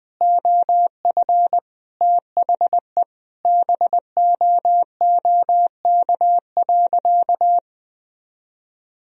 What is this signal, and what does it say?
Morse code, heard as a keyed tone, about 21 wpm, 700 Hz. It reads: OF THE BOOK.
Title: Sentences from Top 1000 Words - Review Part 1 - 20wpm